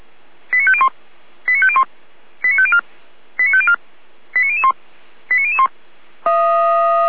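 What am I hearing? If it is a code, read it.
Five-tone selective call (ZVEI), heard as quick runs of five tones, tones of about 70 ms. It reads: 89501 89501 89504 89504 89001 89001
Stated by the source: ZVEI five-tone selective call 89325 received by SDR.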